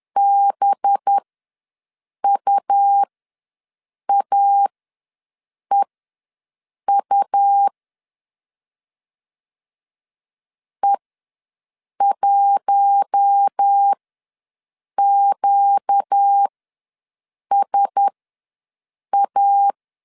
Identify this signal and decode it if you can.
Morse code, heard as a keyed tone, about 10 wpm, spaced as for 6 wpm, 800 Hz. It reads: BUAEU E1QSA
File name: LEZIONI / 5 LBVC / 5 LBV ws 10wpm 3645